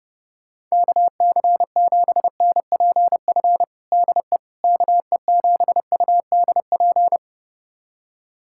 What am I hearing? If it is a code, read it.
Morse code, heard as a keyed tone, about 30 wpm, 700 Hz. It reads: KC7NPF DE KE7UDP
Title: USA Callsign DE Callsign - 6 Characters - 30wpm